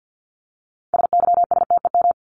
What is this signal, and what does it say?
Morse code, heard as a keyed tone, about 50 wpm, 700 Hz. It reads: SYSTEM
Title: Top 501-750 Words 50wpm